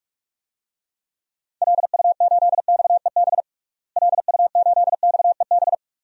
Morse code, heard as a keyed tone, about 43 wpm, 700 Hz. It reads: LU8XEB LU8XEB
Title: DXCC-100 Call Signs - 6 Characters - Repeated 2x - 45wpm